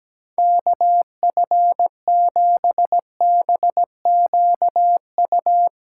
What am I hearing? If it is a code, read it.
Morse code, heard as a keyed tone, about 17 wpm, 700 Hz. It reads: KF7BQU
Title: USA Callsign DE Callsign 17wpm